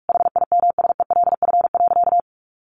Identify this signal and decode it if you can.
Morse code, heard as a keyed tone, about 44 wpm, 700 Hz. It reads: HIMSELF.
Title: Additional Easy Phrases 45wpm